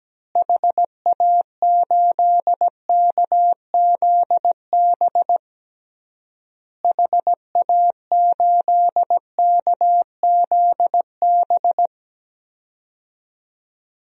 Morse code, heard as a keyed tone, about 17 wpm, 700 Hz. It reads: HA8KZB HA8KZB
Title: DXCC-100 Call Signs - 6 Characters - Repeated 3x - 17wpm